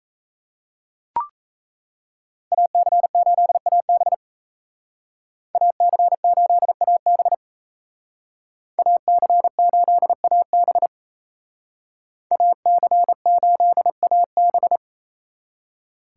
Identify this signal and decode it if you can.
Morse code, heard as a keyed tone, about 35 wpm, 700 Hz. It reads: AC8AB AC8AB AC8AB AC8AB
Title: Speed Racing - USA Call Signs 28wpm